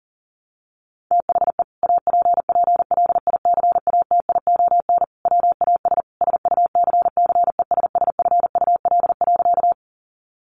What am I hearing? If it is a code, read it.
Morse code, heard as a keyed tone, about 40 wpm, 700 Hz. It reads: THE APPLICATION WAS SUCCESSFUL.
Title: Sentences from Top 1200 Words 40wpm